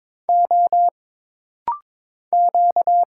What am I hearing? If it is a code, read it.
Morse code, heard as a keyed tone, about 22 wpm, 700 Hz. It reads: O Q
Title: Single Letter-Number Alt - Rapid-Fire 22wpm